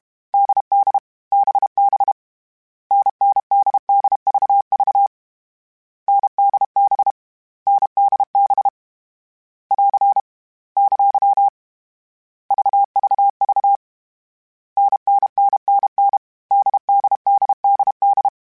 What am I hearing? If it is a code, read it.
Morse code, heard as a keyed tone, about 32 wpm, 800 Hz. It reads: DD BB NNDDVV NDB NDB <AR> ! VVV NNNNN DDDDD